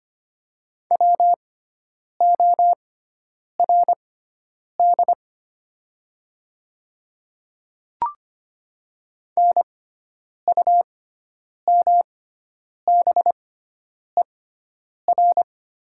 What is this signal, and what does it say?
Morse code, heard as a keyed tone, about 25 wpm, spaced as for 9 wpm, 700 Hz. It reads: WORD NUMBER
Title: 6x Character Spacing - Top 100 Words - 25wpm